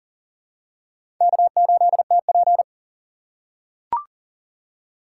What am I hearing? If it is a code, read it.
Morse code, heard as a keyed tone, about 40 wpm, 700 Hz. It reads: K8TP
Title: USA Call Signs - 4x Character Spacing - 40wpm